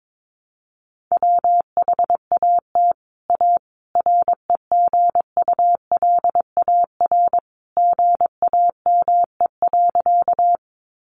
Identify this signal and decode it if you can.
Morse code, heard as a keyed tone, about 22 wpm, 700 Hz. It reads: WHAT A REGULAR GAME.
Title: Sentences from Top 900 Words 22wpm